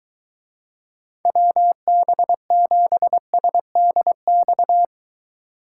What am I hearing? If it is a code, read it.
Morse code, heard as a keyed tone, about 23 wpm, 700 Hz. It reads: WB7SDX